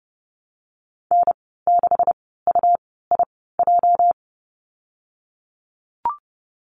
Morse code, heard as a keyed tone, about 30 wpm, spaced as for 17 wpm, 700 Hz. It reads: N6UIJ
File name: Lesson 37g - Letter x - Call Signs 30wpm